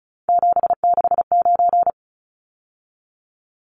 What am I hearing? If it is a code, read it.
Morse code, heard as a keyed tone, about 34 wpm, 700 Hz. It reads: 769
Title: Numbers - 3-Digit 35wpm